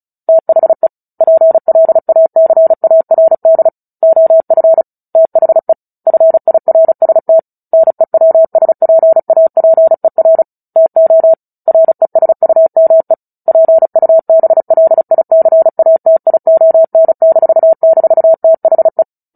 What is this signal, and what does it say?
Morse code, heard as a keyed tone, about 34 wpm, 650 Hz. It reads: THE PLACARD OF THE FIRST NEWSPAPER TO RESUME PUBLICATION--THE
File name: Book Two / 09-Chapter 9 - 35 WPM